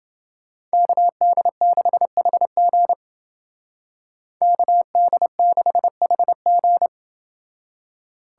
Morse code, heard as a keyed tone, about 30 wpm, 700 Hz. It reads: KD6HG KD6HG